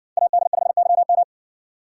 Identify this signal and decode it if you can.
Morse code, heard as a keyed tone, about 60 wpm, 700 Hz. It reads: AD5YM